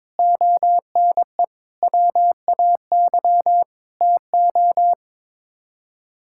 Morse code, heard as a keyed tone, about 21 wpm, 700 Hz. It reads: ONE WAY TO
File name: practice / 22wpm / Sets of 7 Words - 22wpm